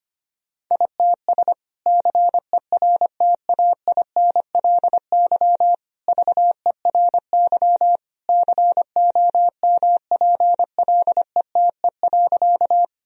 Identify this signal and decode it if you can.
Morse code, heard as a keyed tone, about 25 wpm, 700 Hz. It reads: ITS CERTAINLY VERY COMPLETE.
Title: Sentences from Top 1000 Words - Review Part 2 - 25wpm